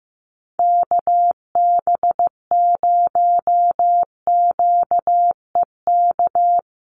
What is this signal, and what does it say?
Morse code, heard as a keyed tone, about 15 wpm, 700 Hz. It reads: KB0QEK
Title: USA Call Signs Repeated 2x 15wpm